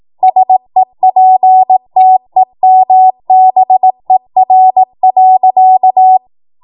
Morse code, heard as a keyed tone, about 18 wpm, 750 Hz. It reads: SEPTEMBER.
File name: Morse / 18WPM